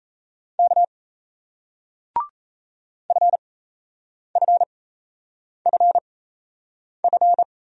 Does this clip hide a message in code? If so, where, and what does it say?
Yes: Morse code from 0.6 to 7.4 s, heard as a keyed tone, about 37 wpm, spaced as for 8 wpm, 700 Hz. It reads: K FFFF